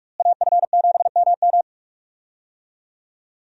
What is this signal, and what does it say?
Morse code, heard as a keyed tone, about 41 wpm, 700 Hz. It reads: AF7MM
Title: USA Call Signs - 5 Characters - Repeated 2x - 45wpm